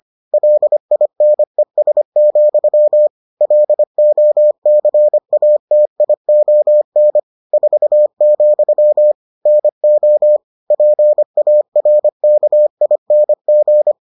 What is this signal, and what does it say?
Morse code, heard as a keyed tone, about 25 wpm, 600 Hz. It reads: LINES, LOCATION 4, NO PARKING